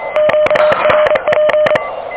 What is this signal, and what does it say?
Morse code, heard as a keyed tone, about 23 wpm, 610 Hz, with noise. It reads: QRP
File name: OK QRP 2023 / OK QRP 2023 OK1DWF